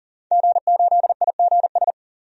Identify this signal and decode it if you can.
Morse code, heard as a keyed tone, about 37 wpm, 700 Hz. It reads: G8IGS
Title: DXCC-100 Call Signs - 5 Characters - 40wpm